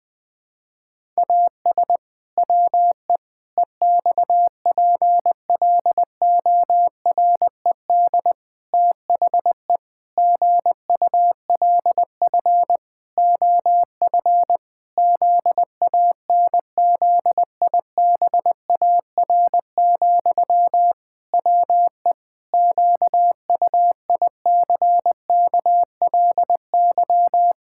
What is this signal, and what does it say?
Morse code, heard as a keyed tone, about 20 wpm, 700 Hz. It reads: AS WE EXPLORED THE GULF OF ZANZIBAR, WE QUICKLY
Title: Pangram Challenge v2 - 20wpm